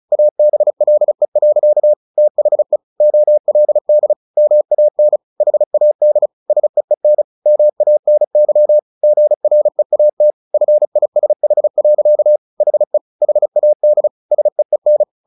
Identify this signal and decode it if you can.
Morse code, heard as a keyed tone, about 35 wpm, 600 Hz. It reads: ABLE. THE OLD MAN HAD SEEN MANY GREAT FISH. HE HAD SEEN
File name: the old man and the sea / the old man and the sea 1 35wpm